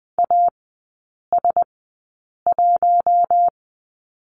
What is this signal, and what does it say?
Morse code, heard as a keyed tone, about 20 wpm, spaced as for 8 wpm, 700 Hz. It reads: AS1
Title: 053 Lesson 10e - Letter r - Three Characters 20wpm